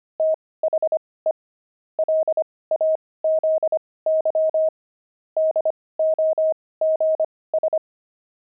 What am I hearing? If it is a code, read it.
Morse code, heard as a keyed tone, about 25 wpm, spaced as for 18 wpm, 630 Hz. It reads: THE LAZY DOGS
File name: pangram-fs-18-cs-25-fox